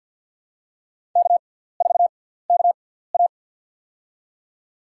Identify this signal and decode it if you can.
Morse code, heard as a keyed tone, about 48 wpm, spaced as for 17 wpm, 700 Hz. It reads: K4KA
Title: USA Call Signs - 4 Characters - 6x Character Spacing - 50wpm